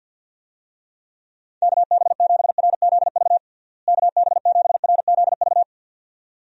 Morse code, heard as a keyed tone, about 46 wpm, 700 Hz. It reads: KB7RZV KB7RZV
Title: USA Call Signs - 6 Characters - Repeated 2x - 50wpm